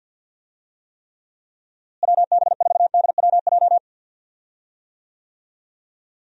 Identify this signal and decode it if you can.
Morse code, heard as a keyed tone, about 51 wpm, 700 Hz. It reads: WB4DWJ
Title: USA Call Signs - 6 Characters - Repeated 2x - 50wpm